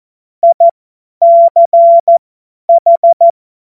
Morse code, heard as a keyed tone, about 14 wpm, spaced as for 10 wpm, 680 Hz. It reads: ICH
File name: Morsekurs / Witze / Witz 01. Juni 2021, 50 F70